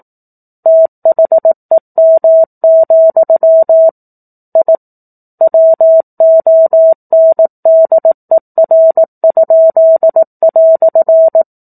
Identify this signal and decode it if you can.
Morse code, heard as a keyed tone, about 18 wpm, 650 Hz. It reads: THEM, I WONDER?"
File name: 04-Chapter-04 18 WPM